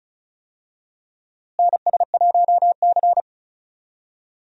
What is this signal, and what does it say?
Morse code, heard as a keyed tone, about 36 wpm, 700 Hz. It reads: NS1C
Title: USA Call Signs - 4 Characters - Repeated 3x - 35wpm